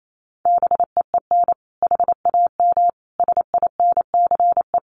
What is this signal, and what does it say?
Morse code, heard as a keyed tone, about 28 wpm, 700 Hz. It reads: BEEN HAM SINCE
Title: Common QSO Phrases 28wpm